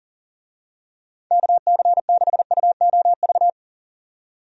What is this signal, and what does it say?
Morse code, heard as a keyed tone, about 40 wpm, 700 Hz. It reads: KC6UOV
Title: USA Call Signs - 6 Characters - 40wpm